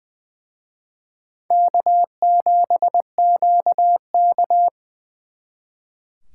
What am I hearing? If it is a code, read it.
Morse code, heard as a keyed tone, about 24 wpm, 700 Hz. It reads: K7QK
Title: Speed Racing - USA 1x2 Call Signs 20wpm